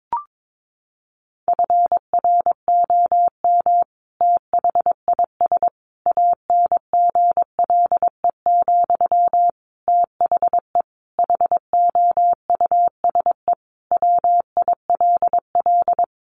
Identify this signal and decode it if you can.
Morse code, heard as a keyed tone, about 22 wpm, 700 Hz. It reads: FROM THIS ANGLE, THE HOUSE WILL
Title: Sentences from Top 1500 Words 22wpm